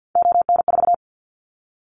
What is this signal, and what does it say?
Morse code, heard as a keyed tone, about 50 wpm, 700 Hz. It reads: ON4
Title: Callsign Prefixes for Top 50 DXCC - 50wpm